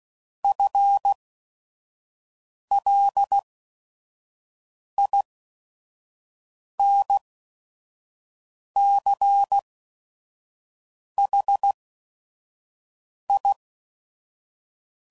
Morse code, heard as a keyed tone, about 16 wpm, spaced as for 5 wpm, 800 Hz. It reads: FLINCHI
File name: Suffixes / Special ING's / VerySlowToFaster ING's